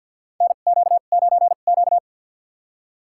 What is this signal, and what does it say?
Morse code, heard as a keyed tone, about 53 wpm, spaced as for 37 wpm, 700 Hz. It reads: NQ9Q
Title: USA Call Signs - 4 Characters - 2x Character Spacing - 50wpm